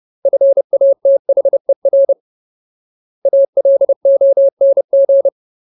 Morse code, heard as a keyed tone, about 30 wpm, 550 Hz. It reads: FATHER ALONG
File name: Ultimate Word List / 30WPM Ultimate